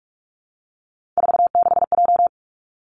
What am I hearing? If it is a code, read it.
Morse code, heard as a keyed tone, about 46 wpm, 700 Hz. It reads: 46J
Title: Three Challenging Letter-Number Combinations 45wpm